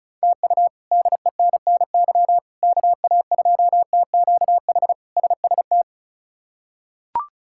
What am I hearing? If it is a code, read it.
Morse code, heard as a keyed tone, about 36 wpm, 700 Hz. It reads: TU DENNY KA2TQH SST
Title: K1USN SST QSO Practice - 35wpm